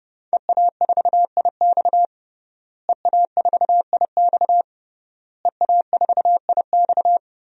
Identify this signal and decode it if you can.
Morse code, heard as a keyed tone, about 30 wpm, 700 Hz. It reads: EA4IX EA4IX EA4IX
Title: DXCC-100 Call Signs - 5 Characters - Repeated 3x - 30wpm